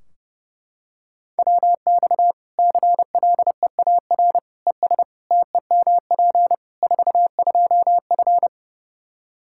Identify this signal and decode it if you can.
Morse code, heard as a keyed tone, about 30 wpm, 700 Hz. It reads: WX CLEAR ES TEMP 42F
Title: Practice QSO - Exchange 2 - Info - 30wpm